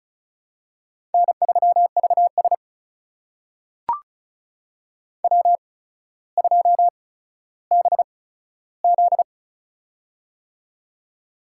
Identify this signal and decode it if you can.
Morse code, heard as a keyed tone, about 36 wpm, spaced as for 15 wpm, 700 Hz. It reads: N3VS W2BZ
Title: USA Call Signs - 4 Characters - 8x Character Spacing - 35wpm